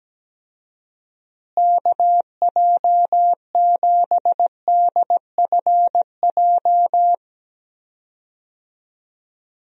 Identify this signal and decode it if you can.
Morse code, heard as a keyed tone, about 17 wpm, 700 Hz. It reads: KJ7DFJ
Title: USA Call Signs - 6 Characters - Repeated 2x - 17wpm